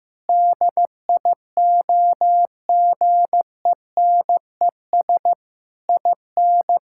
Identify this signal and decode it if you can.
Morse code, heard as a keyed tone, about 15 wpm, 700 Hz. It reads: DIOGENES IN
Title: Pangram Challenge v2 - 15wpm